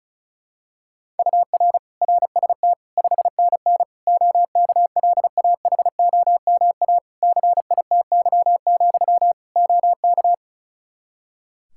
Morse code, heard as a keyed tone, about 36 wpm, 700 Hz. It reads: UR RST 5NN OKLAHOMA CITY, OK
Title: Practice QSO - Exchange 1 - Beginning - 35wpm